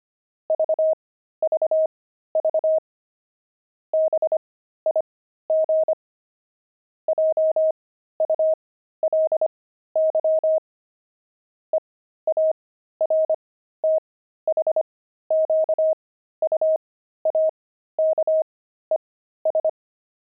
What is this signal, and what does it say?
Morse code, heard as a keyed tone, about 25 wpm, spaced as for 13 wpm, 630 Hz. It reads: VVV BIG JULY EARTHQUAKES
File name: pangram-fs-13-cs-25-zany